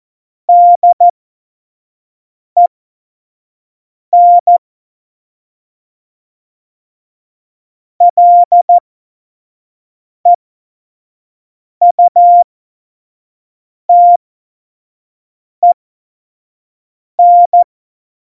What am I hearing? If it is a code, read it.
Morse code, heard as a keyed tone, about 14 wpm, spaced as for 5 wpm, 700 Hz. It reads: DEN LEUTEN
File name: Morsekurs / Witze / Witz 15. Dezember, F70